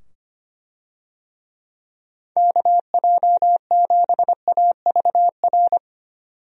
Morse code, heard as a keyed tone, about 25 wpm, 700 Hz. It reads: KJ7AVR
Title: USA Call Signs - 6 Characters - 25wpm